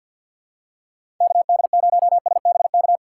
Morse code, heard as a keyed tone, about 53 wpm, 700 Hz. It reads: KD0SBK
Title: USA Call Signs - 4x Character Spacing - 50wpm